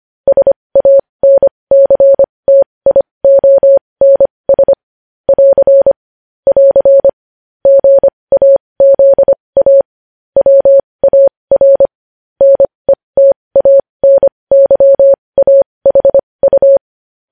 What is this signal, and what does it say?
Morse code, heard as a keyed tone, about 25 wpm, spaced as for 20 wpm, 550 Hz. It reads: SANCTIONS <AR> <AR> GAZA WAR NETANYAHU
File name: podcast / News headlines-1x-25x20